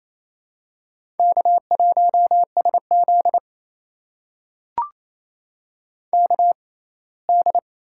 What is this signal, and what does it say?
Morse code, heard as a keyed tone, about 28 wpm, spaced as for 19 wpm, 700 Hz. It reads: K1SZ KD
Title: USA Call Signs - 4 Characters - 6x Character Spacing - 28wpm